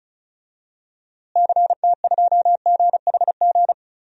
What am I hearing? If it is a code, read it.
Morse code, heard as a keyed tone, about 36 wpm, 700 Hz. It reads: CT2GHG
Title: DXCC-100 Call Signs - 6 Characters - Repeated 3x - 35wpm